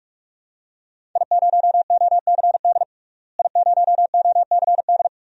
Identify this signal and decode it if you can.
Morse code, heard as a keyed tone, about 44 wpm, 700 Hz. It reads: I0OCD I0OCD
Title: DXCC-100 Call Signs - 5 Characters - Repeated 2x - 45wpm